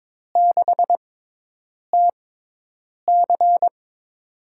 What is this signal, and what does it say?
Morse code, heard as a keyed tone, about 22 wpm, spaced as for 8 wpm, 700 Hz. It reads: 6TC
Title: USA Call Signs - 5 Characters - 6x Character Spacing - 22wpm